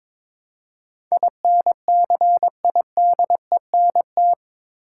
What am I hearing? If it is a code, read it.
Morse code, heard as a keyed tone, about 22 wpm, 700 Hz. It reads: INCIDENT
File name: practice / 22wpm / Eight Letter Words - 22wpm